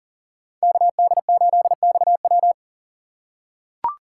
Morse code, heard as a keyed tone, about 40 wpm, 700 Hz. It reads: KD8XW
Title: USA Call Signs - 5 Characters - 6x Character Spacing - 40wpm